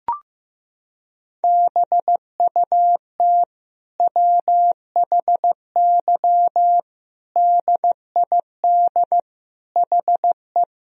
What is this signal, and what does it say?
Morse code, heard as a keyed tone, about 15 wpm, 700 Hz. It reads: BUT WHY DID HE
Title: Sentences from Top 1000 Words - Review Part 2 - 15wpm